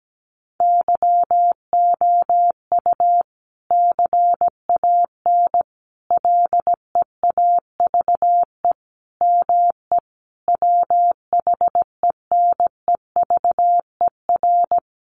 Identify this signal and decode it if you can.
Morse code, heard as a keyed tone, about 17 wpm, 700 Hz. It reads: YOU CAN LEAVE ME WHENEVER
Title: Sentences from Top 1200 Words 17wpm